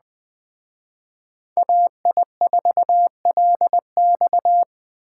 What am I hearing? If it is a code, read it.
Morse code, heard as a keyed tone, about 19 wpm, 700 Hz. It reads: AI4LX